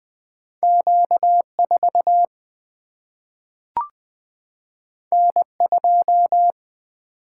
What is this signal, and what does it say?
Morse code, heard as a keyed tone, about 20 wpm, 700 Hz. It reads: Q4 N2
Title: Single-Letter Single-Number 20wpm